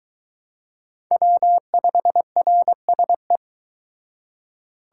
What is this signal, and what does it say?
Morse code, heard as a keyed tone, about 22 wpm, 700 Hz. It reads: W5RSE